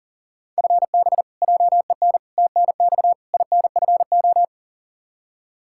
Morse code, heard as a keyed tone, about 40 wpm, 700 Hz. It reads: FB JEN TNX INFO
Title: Practice QSO - Exchange 3 - More Info - 40wpm